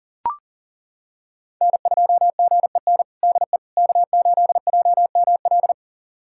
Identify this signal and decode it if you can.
Morse code, heard as a keyed tone, about 40 wpm, 700 Hz. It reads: N2GEN DE K8JML
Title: USA Callsign DE Callsign - 5 Characters - 40wpm